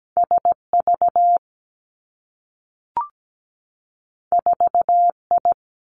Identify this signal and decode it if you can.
Morse code, heard as a keyed tone, about 17 wpm, 700 Hz. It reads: SV 4I
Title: Two Challenging Letter-Number Combinations 17wpm